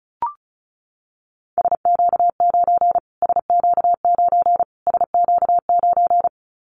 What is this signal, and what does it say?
Morse code, heard as a keyed tone, about 34 wpm, 700 Hz. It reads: SQ9 SQ9 SQ9
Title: Callsign Prefixes for Top 25 DXCC - Repeated 3x - 35wpm